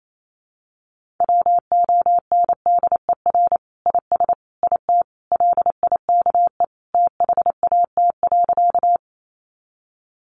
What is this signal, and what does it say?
Morse code, heard as a keyed tone, about 28 wpm, 700 Hz. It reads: WONDER IS IT LIKE THAT.
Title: Sentences from Top 500 Words 28wpm